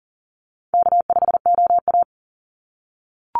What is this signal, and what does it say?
Morse code, heard as a keyed tone, about 40 wpm, 700 Hz. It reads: K5OA
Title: Easy Call Signs - 40wpm